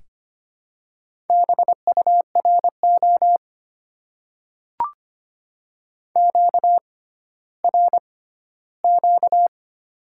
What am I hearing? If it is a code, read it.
Morse code, heard as a keyed tone, about 25 wpm, spaced as for 14 wpm, 690 Hz. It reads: BURO QRQ
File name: Top 90 QSO Elements - 6x Character Spacing - 25wpm